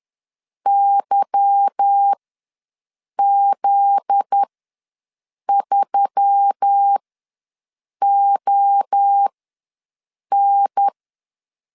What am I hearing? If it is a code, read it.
Morse code, heard as a keyed tone, about 10 wpm, spaced as for 6 wpm, 800 Hz. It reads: YZ3ON
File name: LEZIONI / 8 YJ45 / 8 YJ4 ws 10wpm 4020